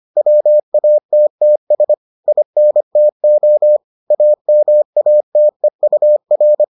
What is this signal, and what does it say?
Morse code, heard as a keyed tone, about 25 wpm, 600 Hz. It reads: WATTS INTO AMATEUR